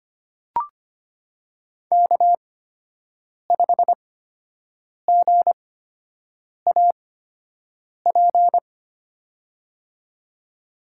Morse code, heard as a keyed tone, about 25 wpm, spaced as for 7 wpm, 700 Hz. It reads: K5GAP
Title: USA Call Signs - 5 Characters - 8x Character Spacing - 25wpm